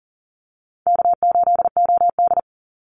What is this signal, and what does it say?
Morse code, heard as a keyed tone, about 38 wpm, 700 Hz. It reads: K8OD
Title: Easy Call Signs - 40wpm